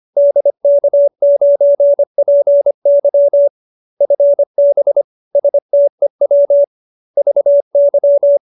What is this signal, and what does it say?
Morse code, heard as a keyed tone, about 25 wpm, 570 Hz. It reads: DK9PY FB STEW VY
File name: QSO318 25